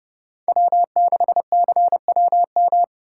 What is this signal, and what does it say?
Morse code, heard as a keyed tone, about 30 wpm, 700 Hz. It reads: W6CWM
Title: K1USN SST QSO Practice - 30wpm